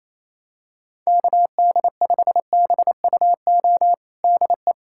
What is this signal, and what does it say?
Morse code, heard as a keyed tone, about 28 wpm, 700 Hz. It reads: KD5BUO DE
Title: USA Callsign DE Callsign - 6 Characters - 28wpm